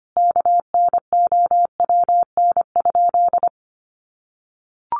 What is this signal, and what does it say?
Morse code, heard as a keyed tone, about 25 wpm, 700 Hz. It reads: KNOWN?
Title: Sentences from Top 700 Words 25wpm